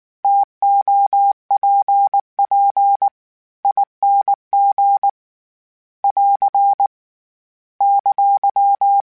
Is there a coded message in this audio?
Yes: Morse code from 0.2 to 9.0 s, heard as a keyed tone, about 19 wpm, 800 Hz. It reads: TOPP ING <AR> !